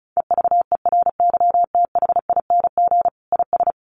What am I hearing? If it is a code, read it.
Morse code, heard as a keyed tone, about 34 wpm, 700 Hz. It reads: EVERYTHING IS
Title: Sentences from Top 1400 Words 35wpm